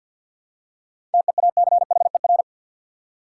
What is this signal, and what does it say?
Morse code, heard as a keyed tone, about 50 wpm, 700 Hz. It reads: TEACHER
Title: Seven Letter Words - 50wpm